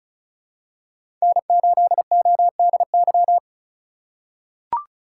Morse code, heard as a keyed tone, about 36 wpm, 700 Hz. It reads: N8ODY
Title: USA Call Signs - 5 Characters - 2x Character Spacing - 35wpm